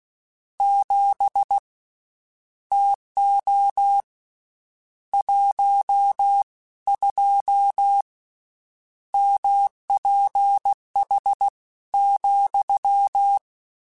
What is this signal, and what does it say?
Morse code, heard as a keyed tone, about 16 wpm, spaced as for 14 wpm, 800 Hz. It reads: 7 TO 12 MPH,
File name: Spaced Characters / Spaced Characters-mp3s / Weather 5 22 sp @15